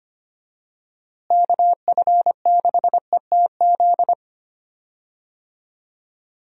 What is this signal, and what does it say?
Morse code, heard as a keyed tone, about 25 wpm, 700 Hz. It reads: KF6ETZ
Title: USA Call Signs - 6 Characters - Repeated 3x - 25wpm